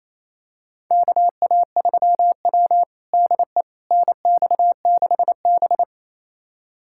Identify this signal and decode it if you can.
Morse code, heard as a keyed tone, about 27 wpm, 690 Hz. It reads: KA3W DE NX6B
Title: USA Callsign DE Callsign - 4 Characters - 28wpm